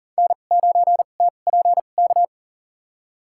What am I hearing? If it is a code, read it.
Morse code, heard as a keyed tone, about 40 wpm, spaced as for 29 wpm, 700 Hz. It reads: N9TPK
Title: USA Call Signs - 5 Characters - 2x Character Spacing - 40wpm